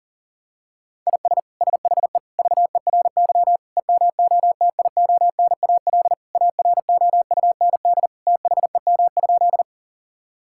Morse code, heard as a keyed tone, about 37 wpm, 700 Hz. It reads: IS SHE VERY EMOTIONAL AROUND THEM?
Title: Sentences from Top 1000 Words - Review Part 5 - 40wpm